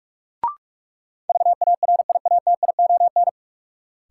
Morse code, heard as a keyed tone, about 46 wpm, 700 Hz. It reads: VARIATION
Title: Sentences from Top 1600 Words - New Word Drill Alt - 45wpm